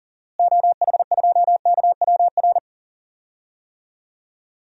Morse code, heard as a keyed tone, about 38 wpm, 700 Hz. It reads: OH2KWR
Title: DXCC-100 Call Signs - 6 Characters - Repeated 2x - 40wpm